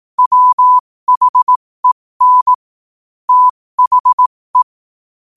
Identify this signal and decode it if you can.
Morse code, heard as a keyed tone, about 18 wpm, spaced as for 15 wpm, 1000 Hz. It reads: WHEN THE
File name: qotdfiles / 15wpm